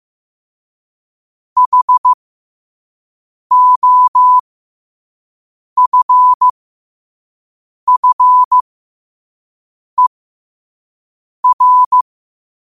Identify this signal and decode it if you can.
Morse code, heard as a keyed tone, about 15 wpm, spaced as for 5 wpm, 1000 Hz. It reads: HOFFER
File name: qotdfiles / 5wpm